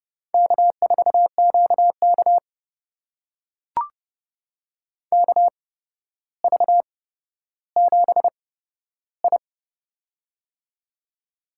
Morse code, heard as a keyed tone, about 30 wpm, spaced as for 13 wpm, 700 Hz. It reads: K4QK KV7I
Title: USA Call Signs - 4 Characters - 8x Character Spacing - 30wpm